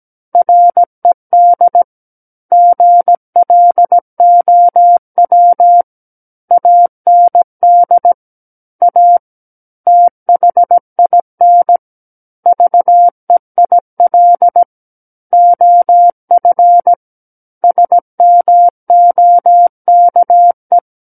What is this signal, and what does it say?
Morse code, heard as a keyed tone, about 17 wpm, 700 Hz. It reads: RED GLOW AND A THIN VEIL OF SMOKE
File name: Book One / 08-Chapter 8 - 17 WPM